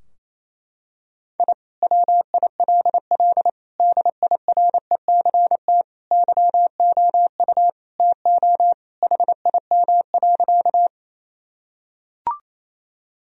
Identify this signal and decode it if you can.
Morse code, heard as a keyed tone, about 28 wpm, 700 Hz. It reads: I WILL DIRECT YOU TO HIM.
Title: Sentences from Top 1000 Words - Review Part 2 - 28wpm